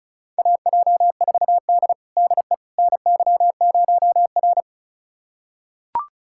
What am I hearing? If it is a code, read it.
Morse code, heard as a keyed tone, about 36 wpm, 700 Hz. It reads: AJ4D DE NY0R
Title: USA Callsign DE Callsign - 4 Characters - 35wpm